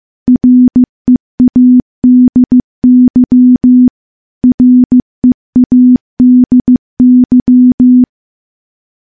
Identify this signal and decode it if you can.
Morse code, heard as a keyed tone, about 15 wpm, 250 Hz. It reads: READY READY
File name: morse-training-15WPM